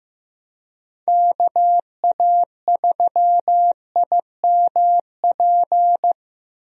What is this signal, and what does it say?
Morse code, heard as a keyed tone, about 15 wpm, 700 Hz. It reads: KA3IMP
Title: USA Call Signs - 6 Characters - 15wpm